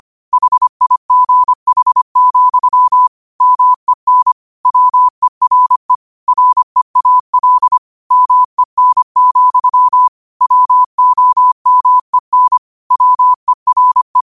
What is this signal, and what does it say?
Morse code, heard as a keyed tone, about 25 wpm, 1000 Hz. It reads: HIGH, MEN WERE REAL MEN, WOMEN WERE
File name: qotdfiles / 25wpm